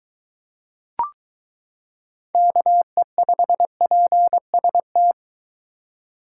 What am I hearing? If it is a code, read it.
Morse code, heard as a keyed tone, about 22 wpm, 700 Hz. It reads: KE5PST